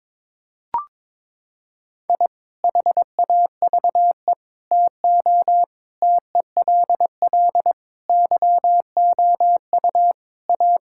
Morse code, heard as a keyed tone, about 21 wpm, 700 Hz. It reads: I HAVE TO TELL YOU A
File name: practice / 22wpm / Sentences with 3-Word Phrases - 22wpm